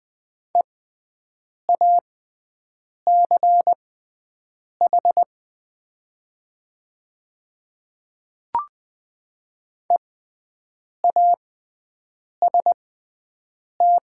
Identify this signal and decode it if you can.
Morse code, heard as a keyed tone, about 20 wpm, spaced as for 7 wpm, 700 Hz. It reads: EACH EAST
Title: 6x Character Spacing - Top 500 Words - 20wpm